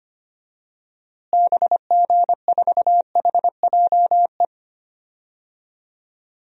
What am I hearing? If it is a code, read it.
Morse code, heard as a keyed tone, about 25 wpm, 700 Hz. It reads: BG4HJE
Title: DXCC-100 Call Signs - 6 Characters - Repeated 3x - 25wpm